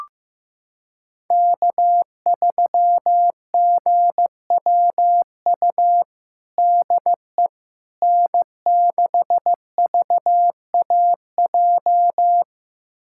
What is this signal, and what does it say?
Morse code, heard as a keyed tone, about 15 wpm, 700 Hz. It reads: K3GWU DE N6VAJ
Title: USA Callsign DE Callsign - 5 Characters - 15wpm